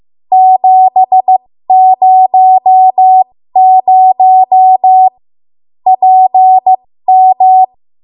Morse code, heard as a keyed tone, about 15 wpm, spaced as for 13 wpm, 750 Hz. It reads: 700 PM
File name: Morse / 13WPM